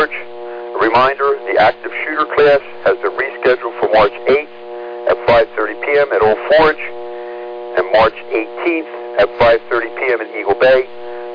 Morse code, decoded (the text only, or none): none